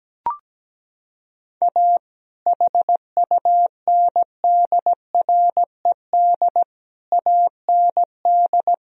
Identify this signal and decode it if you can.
Morse code, heard as a keyed tone, about 17 wpm, 700 Hz. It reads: A HUNDRED AND
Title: Sentences from Top 1000 Words - Review Part 2 - 17wpm